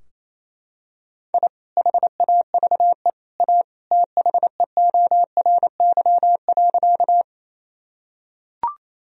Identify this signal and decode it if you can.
Morse code, heard as a keyed tone, about 28 wpm, 700 Hz. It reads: I HAVE A THEORY.
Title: Sentences from Top 1000 Words - Review Part 3 - 28wpm